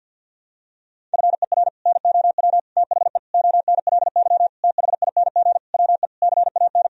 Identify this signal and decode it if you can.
Morse code, heard as a keyed tone, about 50 wpm, 700 Hz. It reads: FER NOW THE ONLY THING WE CAN
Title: Sentences from Top 200 Words and Abbreviations 50wpm